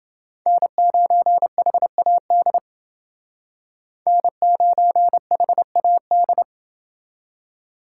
Morse code, heard as a keyed tone, about 28 wpm, 700 Hz. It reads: N9HAD N9HAD